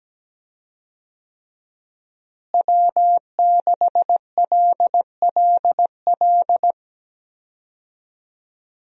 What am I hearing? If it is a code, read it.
Morse code, heard as a keyed tone, about 17 wpm, 700 Hz. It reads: W6LLL
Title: USA Call Signs - 5 Characters - Repeated 3x - 17wpm